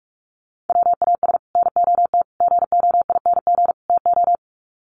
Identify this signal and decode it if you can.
Morse code, heard as a keyed tone, about 44 wpm, 700 Hz. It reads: WAS NOT GOING TO
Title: Sets of 4 Words 45wpm